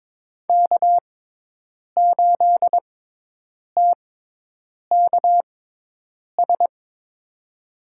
Morse code, heard as a keyed tone, about 21 wpm, spaced as for 8 wpm, 700 Hz. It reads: K8TKS
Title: USA Call Signs - 5 Characters - 6x Character Spacing - 22wpm